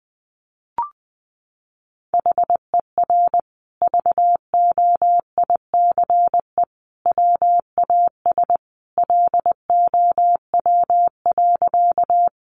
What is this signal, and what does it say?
Morse code, heard as a keyed tone, about 19 wpm, 700 Hz. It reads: HER VOICE WAS LOW.
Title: Sentences from Top 500 Words 20wpm